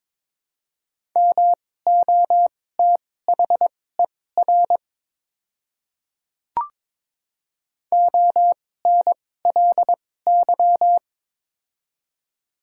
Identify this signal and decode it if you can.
Morse code, heard as a keyed tone, about 22 wpm, spaced as for 16 wpm, 700 Hz. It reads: MOTHER ONLY
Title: 2x Character Spacing - Top 200 Words - 22wpm